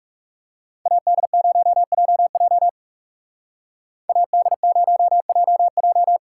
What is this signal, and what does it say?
Morse code, heard as a keyed tone, about 36 wpm, 700 Hz. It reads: AD0JJ AD0JJ
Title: Speed Racing - USA Call Signs 30wpm